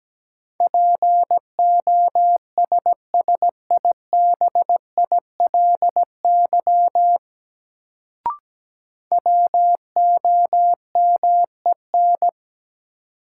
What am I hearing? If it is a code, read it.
Morse code, heard as a keyed tone, about 17 wpm, 700 Hz. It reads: POSSIBILY WOMEN
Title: Sentences from Top 1300 Words - New Word Drill - 17wpm